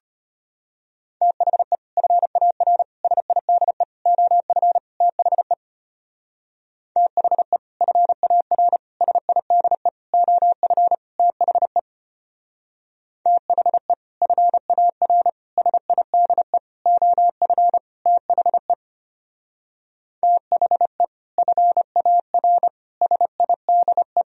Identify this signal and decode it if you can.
Morse code, heard as a keyed tone, about 31 wpm, 700 Hz. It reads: THE FAR SIDE OF THE THE FAR SIDE OF THE THE FAR SIDE OF THE THE FAR SIDE
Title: Speed-Racing - Sets of 5 Words 25wpm